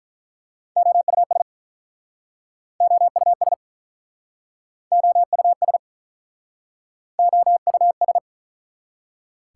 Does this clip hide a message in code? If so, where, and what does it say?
Yes: Morse code from 0.8 to 8.2 s, heard as a keyed tone, about 48 wpm, 700 Hz. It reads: OUS OUS OUS OUS